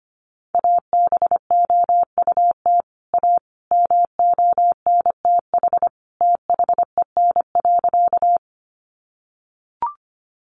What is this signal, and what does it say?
Morse code, heard as a keyed tone, about 25 wpm, 700 Hz. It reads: ABOUT A MONTH THEN.
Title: Sentences from Top 800 Words 25wpm